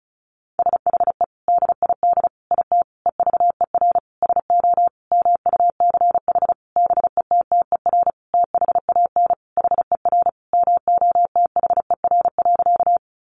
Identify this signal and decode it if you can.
Morse code, heard as a keyed tone, about 34 wpm, 700 Hz. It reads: SHE DID IT EVER SO MUCH BETTER THAN HER MOTHER.
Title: Sentences from Top 400 Words 35wpm